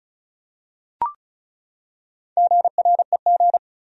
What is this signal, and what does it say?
Morse code, heard as a keyed tone, about 35 wpm, 700 Hz. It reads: GREG
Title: USA Common Names - Max 4 Letters - 35wpm